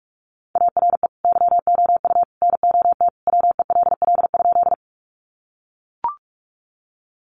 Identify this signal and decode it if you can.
Morse code, heard as a keyed tone, about 44 wpm, 700 Hz. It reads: ARE YOU NOT WELL?
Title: Sentences from Top 600 Words 45wpm